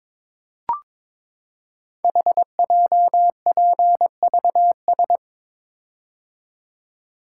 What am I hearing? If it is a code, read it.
Morse code, heard as a keyed tone, about 22 wpm, 700 Hz. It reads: HJPVS
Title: Code Groups - 5 Letters - 22wpm